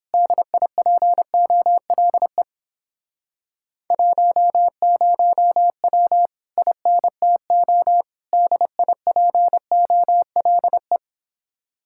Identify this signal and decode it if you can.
Morse code, heard as a keyed tone, about 29 wpm, 700 Hz. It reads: DIPOLE 10W INTO DIPOLE